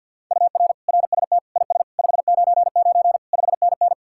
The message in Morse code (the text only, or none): UR RST IS 599 5NN